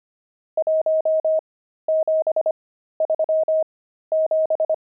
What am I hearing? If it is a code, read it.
Morse code, heard as a keyed tone, about 25 wpm, spaced as for 13 wpm, 630 Hz. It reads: 1737
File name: pangram-fs-13-cs-25-nums9